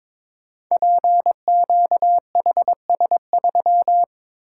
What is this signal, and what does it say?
Morse code, heard as a keyed tone, about 22 wpm, 700 Hz. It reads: PQHS3
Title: Code Groups - 5 Letter:Numbers - 22wpm